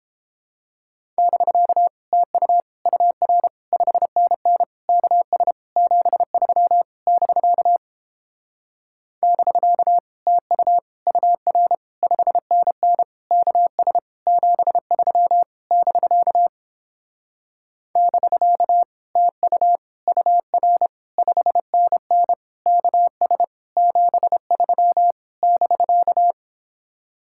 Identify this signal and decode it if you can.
Morse code, heard as a keyed tone, about 29 wpm, 700 Hz. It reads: <BK> TU UR 5NN KS 73 <BK> <BK> TU UR 5NN KS 73 <BK> <BK> TU UR 5NN KS 73 <BK>